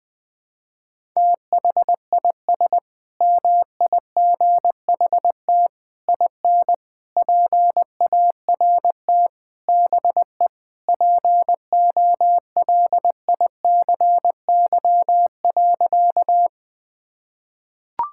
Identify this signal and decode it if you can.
Morse code, heard as a keyed tone, about 20 wpm, 700 Hz. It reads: THIS MIGHT IN PART BE POLICY.
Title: Sentences from Top 1000 Words - Review Part 5 - 20wpm